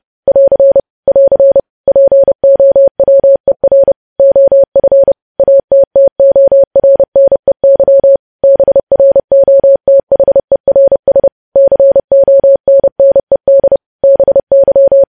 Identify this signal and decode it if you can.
Morse code, heard as a keyed tone, about 30 wpm, 550 Hz. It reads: <AR> <AR> POWER OF ATTORNEY BROTHERS CONNED BY
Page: news headlines in morse code at 30 wpm